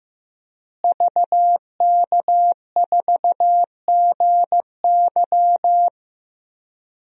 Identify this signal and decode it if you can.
Morse code, heard as a keyed tone, about 15 wpm, 700 Hz. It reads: VK4GY
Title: DXCC-100 Call Signs - 5 Characters - Repeated 2x - 15wpm